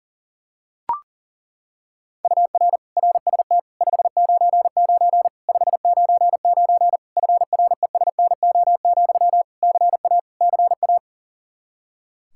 Morse code, heard as a keyed tone, about 40 wpm, 700 Hz. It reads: UR RST 599 599 FRESNO, CA CA
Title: Practice QSO - Exchange 1 - Beginning - 40wpm